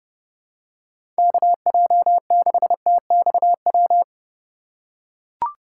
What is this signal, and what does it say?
Morse code, heard as a keyed tone, about 30 wpm, 700 Hz. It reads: KJ6TXW
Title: USA Call Signs - 6 Characters - 3x Character Spacing - 30wpm